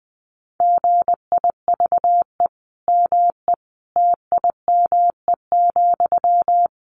Morse code, heard as a keyed tone, about 19 wpm, 700 Hz. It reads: GIVE ME TIME,
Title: Sentences from Top 1300 Words 20wpm